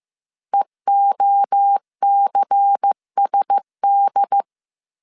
Morse code, heard as a keyed tone, about 15 wpm, 800 Hz. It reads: EOCSD